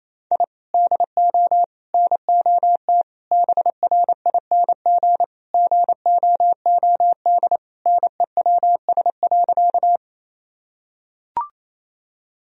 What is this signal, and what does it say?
Morse code, heard as a keyed tone, about 28 wpm, 700 Hz. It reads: I DO NOT BRING GOOD NEWS.
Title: Sentences from Top 1000 Words - Review Part 3 - 28wpm